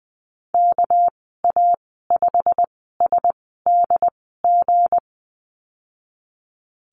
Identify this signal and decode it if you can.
Morse code, heard as a keyed tone, about 19 wpm, spaced as for 14 wpm, 700 Hz. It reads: KA5SDG
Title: 207 Lesson 33g - Letter j - Call Signs 20wpm